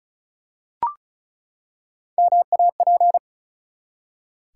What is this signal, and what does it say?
Morse code, heard as a keyed tone, about 36 wpm, 700 Hz. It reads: MAP
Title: Three Letter Words - 35wpm